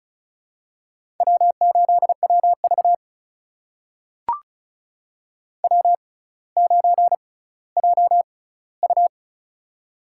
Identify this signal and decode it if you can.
Morse code, heard as a keyed tone, about 36 wpm, spaced as for 18 wpm, 700 Hz. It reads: W8WV W9JU
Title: USA Call Signs - 4 Characters - 6x Character Spacing - 35wpm